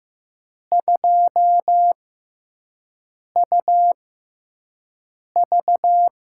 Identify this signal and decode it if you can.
Morse code, heard as a keyed tone, about 15 wpm, spaced as for 5 wpm, 700 Hz. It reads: 2UV
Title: USA Call Signs - 5 Characters - 6x Character Spacing - 15wpm